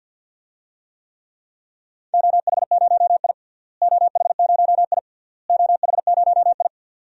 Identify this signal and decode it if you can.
Morse code, heard as a keyed tone, about 53 wpm, 700 Hz. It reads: OH0I OH0I OH0I
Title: DXCC-100 Call Signs - 4 Characters - Repeated 3x - 50wpm